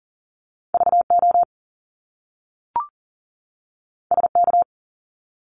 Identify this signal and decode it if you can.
Morse code, heard as a keyed tone, about 41 wpm, 700 Hz. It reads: VO SK
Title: Top Two Letter Combinations in Words 40wpm